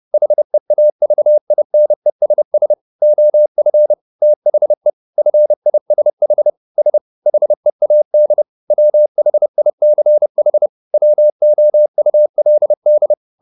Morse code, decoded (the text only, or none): HEAVINESS OF THE FISH S HEAD WHICH WOULD